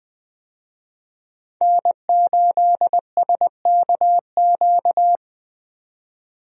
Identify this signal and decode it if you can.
Morse code, heard as a keyed tone, about 20 wpm, 700 Hz. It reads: N8SKQ
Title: USA Call Signs - 5 Characters - 20wpm